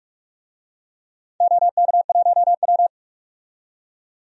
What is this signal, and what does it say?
Morse code, heard as a keyed tone, about 43 wpm, 700 Hz. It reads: OK1W
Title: DXCC-100 Call Signs - 4 Characters - 45wpm